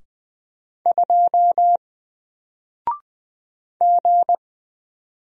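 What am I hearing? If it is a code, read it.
Morse code, heard as a keyed tone, about 20 wpm, 700 Hz. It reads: 2 G
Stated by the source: Single letter and number Morse drill at 20 wpm, from N4LCD.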